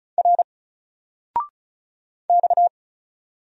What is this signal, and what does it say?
Morse code, heard as a keyed tone, about 36 wpm, 700 Hz. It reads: R X
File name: Single-Letter-Number-35wpm